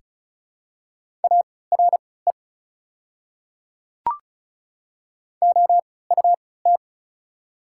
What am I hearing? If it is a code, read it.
Morse code, heard as a keyed tone, about 36 wpm, spaced as for 20 wpm, 700 Hz. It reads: ARE OUT
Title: 3x Character Spacing - Top 100 Words - 35wpm